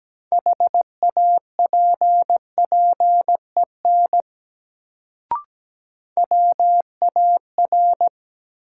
Morse code, heard as a keyed tone, about 17 wpm, 700 Hz. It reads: HAPPEN WAR
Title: Sentences from Top 400 Words - New Word Drill - 17wpm